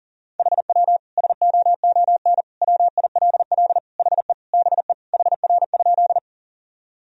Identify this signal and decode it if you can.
Morse code, heard as a keyed tone, about 40 wpm, 700 Hz. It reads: HW SOON WILL HE BE HR?
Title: Sentences from Top 400 Words and Abbreviations 40wpm